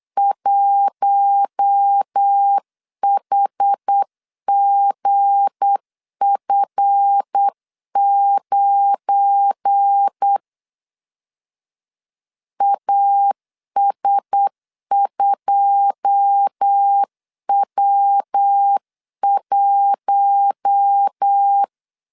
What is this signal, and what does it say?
Morse code, heard as a keyed tone, about 8 wpm, 800 Hz. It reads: 1HGF9 AS2W1
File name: CW / DET / 1 9 1500C S A 8wpm 5637